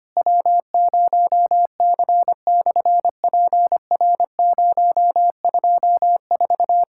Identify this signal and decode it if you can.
Morse code, heard as a keyed tone, about 25 wpm, 700 Hz. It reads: W0C/PR024
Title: SOTA QSO Practice 25wpm